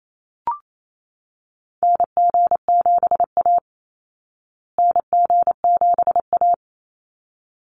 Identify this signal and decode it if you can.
Morse code, heard as a keyed tone, about 28 wpm, 700 Hz. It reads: NG7A NG7A
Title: Call Signs Repeated 3x - 28wpm